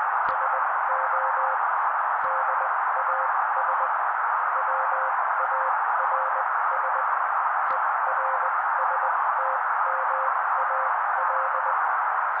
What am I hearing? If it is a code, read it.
Morse code, heard as a keyed tone, about 20 wpm, spaced as for 16 wpm, 530 Hz. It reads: SO DAS WARS ERSTMAL
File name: daily / 16wpm noise